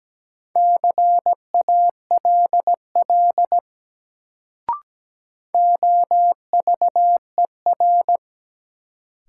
Morse code, heard as a keyed tone, about 17 wpm, 700 Hz. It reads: CALL OVER
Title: Sentences from Top 100 Words - New Word Drill - 17wpm